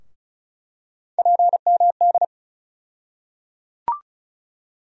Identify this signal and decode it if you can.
Morse code, heard as a keyed tone, about 36 wpm, 700 Hz. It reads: PMD
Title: USA Call Sign Postfixes - 3 Letters 35wpm